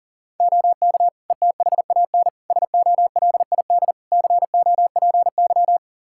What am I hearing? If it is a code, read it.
Morse code, heard as a keyed tone, about 37 wpm, 690 Hz. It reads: OK ETHAN SOLID COPY
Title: Practice QSO - Exchange 3 - More Info - 40wpm